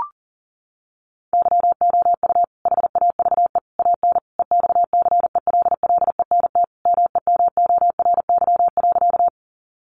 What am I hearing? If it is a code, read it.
Morse code, heard as a keyed tone, about 40 wpm, 700 Hz. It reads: YOU HAVE AN EXCELLENT MEMORY.
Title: Sentences from Top 1300 Words 40wpm